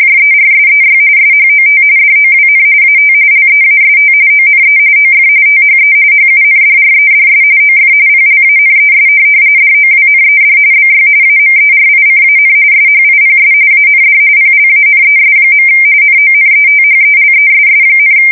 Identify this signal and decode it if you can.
RTTY, about 45.45 baud. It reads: JERSEY SHOT DOWN AT LEAST ONE OF THE HUNDREDS OF PLANES DOWNED BY AMERICAN SHIPS AND PLANES. NONETHELESS,
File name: fsk170 / USS-New-Jersey